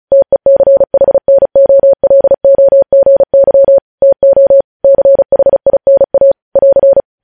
Morse code, heard as a keyed tone, about 36 wpm, 550 Hz. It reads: TECHNOLOGY TO CHINA <AR>
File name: podcast / News headlines-1x-35x35